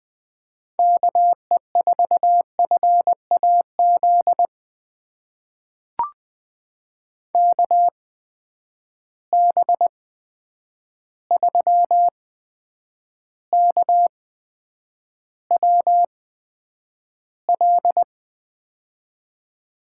Morse code, heard as a keyed tone, about 20 wpm, spaced as for 9 wpm, 700 Hz. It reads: KE4FAZ KB3KWL
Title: USA Call Signs - 6 Characters - 8x Character Spacing - 20wpm